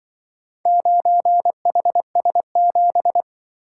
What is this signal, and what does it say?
Morse code, heard as a keyed tone, about 24 wpm, 700 Hz. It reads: 9HS7
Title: Speed-Racing - Four Letter-Number Combinations - 20wpm